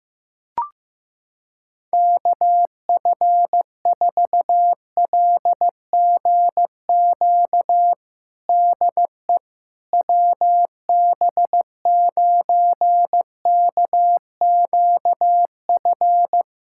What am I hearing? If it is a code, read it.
Morse code, heard as a keyed tone, about 15 wpm, 700 Hz. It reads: KF4LGQ DE WB9KQF
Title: USA Callsign DE Callsign - 6 Characters - 15wpm